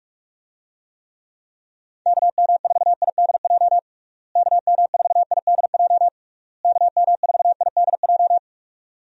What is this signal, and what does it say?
Morse code, heard as a keyed tone, about 46 wpm, 700 Hz. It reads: KM4IDJ KM4IDJ KM4IDJ
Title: USA Call Signs - 6 Characters - Repeated 3x - 45wpm